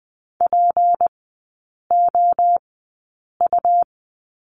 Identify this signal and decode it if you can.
Morse code, heard as a keyed tone, about 20 wpm, spaced as for 8 wpm, 700 Hz. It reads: POU
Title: 152 Lesson 25e - Letter p - Three Characters 20wpm